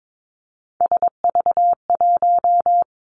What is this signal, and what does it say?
Morse code, heard as a keyed tone, about 22 wpm, 700 Hz. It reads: SV1
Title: Callsign Prefixes for Top 50 DXCC - 22wpm